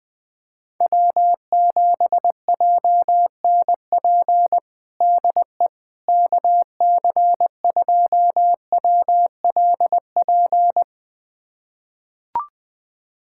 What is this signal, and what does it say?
Morse code, heard as a keyed tone, about 21 wpm, 690 Hz. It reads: W7JNP DE KC2WLP
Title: USA Callsign DE Callsign 20wpm